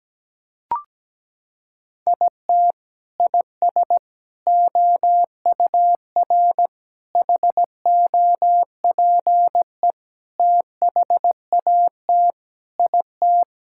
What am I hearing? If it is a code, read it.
Morse code, heard as a keyed tone, about 17 wpm, 700 Hz. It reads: IT IS OUR HOPE THAT IT
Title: Sentences from Top 1000 Words - Review Part 5 - 17wpm